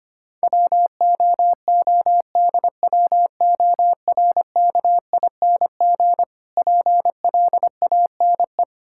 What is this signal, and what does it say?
Morse code, heard as a keyed tone, about 25 wpm, 700 Hz. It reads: WOODWORKING PLANE
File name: practice / 25wpm / Difficult Words - Speed-Racing with Hints - 25wpm